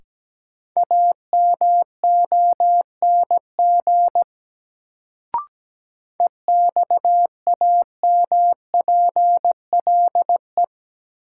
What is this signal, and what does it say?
Morse code, heard as a keyed tone, about 17 wpm, 700 Hz. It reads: AMONG EXAMPLE
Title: Sets of 2 Words - New Word Drill - 17wpm